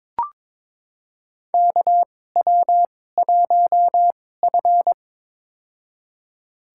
Morse code, heard as a keyed tone, about 21 wpm, spaced as for 16 wpm, 700 Hz. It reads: KW1F
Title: USA Call Signs - 4 Characters - 2x Character Spacing - 22wpm